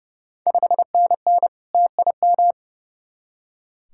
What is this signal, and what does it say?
Morse code, heard as a keyed tone, about 30 wpm, 700 Hz. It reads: 5NN TIM
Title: Ontario QSO Party Exchange 30wpm